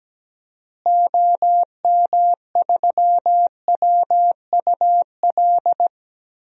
Morse code, heard as a keyed tone, about 17 wpm, 700 Hz. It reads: OM3WUL
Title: DXCC-100 Call Signs - 6 Characters - Repeated 3x - 17wpm